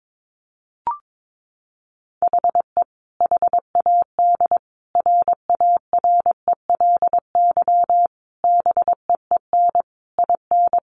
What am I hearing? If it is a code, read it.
Morse code, heard as a keyed tone, about 22 wpm, 700 Hz. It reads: HE HAD RARELY BEEN IN
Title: Sentences from Top 1400 Words 22wpm